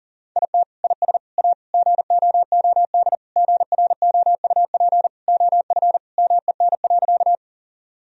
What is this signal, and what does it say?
Morse code, heard as a keyed tone, about 40 wpm, 700 Hz. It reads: IT IS A GOOD GROUP OF MEN.
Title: Sentences from Top 1000 Words - Review Part 1 - 40wpm